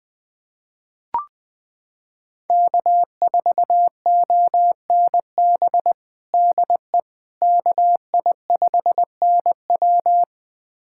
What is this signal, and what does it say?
Morse code, heard as a keyed tone, about 21 wpm, 700 Hz. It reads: K4ONB DE KI5NW
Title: USA Callsign DE Callsign 20wpm